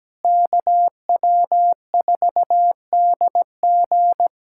Morse code, heard as a keyed tone, about 17 wpm, 700 Hz. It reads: KW4DG
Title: USA Callsign DE Callsign - 5 Characters - 17wpm